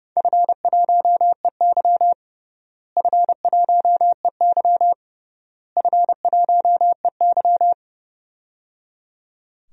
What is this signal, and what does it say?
Morse code, heard as a keyed tone, about 30 wpm, 700 Hz. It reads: F1EY F1EY F1EY
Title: DXCC-100 Call Signs - 4 Characters - Repeated 3x - 30wpm